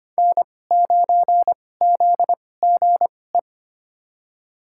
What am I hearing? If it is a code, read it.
Morse code, heard as a keyed tone, about 25 wpm, spaced as for 18 wpm, 700 Hz. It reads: N9ZGE
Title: USA Call Signs - 5 Characters - 2x Character Spacing - 25wpm